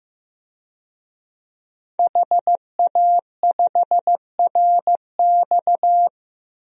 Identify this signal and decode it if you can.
Morse code, heard as a keyed tone, about 15 wpm, 700 Hz. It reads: HA5RX
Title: DXCC-100 Call Signs - 5 Characters - 15wpm